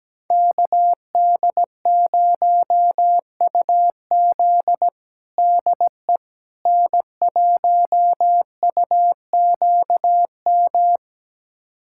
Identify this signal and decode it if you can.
Morse code, heard as a keyed tone, about 17 wpm, 700 Hz. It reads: KD0UZ DE N1UQM
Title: USA Callsign DE Callsign - 5 Characters - 17wpm